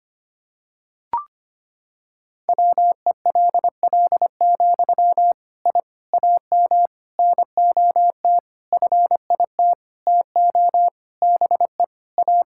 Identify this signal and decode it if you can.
Morse code, heard as a keyed tone, about 25 wpm, 700 Hz. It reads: WELL, I AM NOT FIT TO BE A
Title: Sentences from Top 1000 Words - Review Part 5 - 25wpm